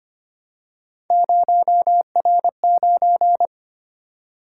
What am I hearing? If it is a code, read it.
Morse code, heard as a keyed tone, about 25 wpm, 700 Hz. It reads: 0R9
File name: Three Letter-Number Combinations - 25wpm